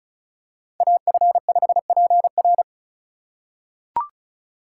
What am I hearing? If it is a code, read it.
Morse code, heard as a keyed tone, about 34 wpm, 700 Hz. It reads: AF5PR
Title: USA Call Signs - 5 Characters - 4x Character Spacing - 35wpm